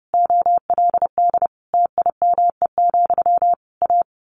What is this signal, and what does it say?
Morse code, heard as a keyed tone, about 30 wpm, 700 Hz. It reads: OLD TIME, A
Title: A Sound Of Thunder 30wpm